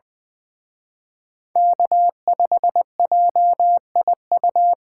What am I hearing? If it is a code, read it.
Morse code, heard as a keyed tone, about 25 wpm, 700 Hz. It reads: K5JIU
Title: Speed Racing - USA 1x3 Call Signs 20wpm